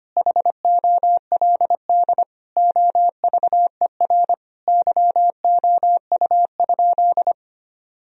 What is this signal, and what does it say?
Morse code, heard as a keyed tone, about 25 wpm, 700 Hz. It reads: HOLD OVER YOU?
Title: Sentences from Top 1000 Words - Review Part 2 - 25wpm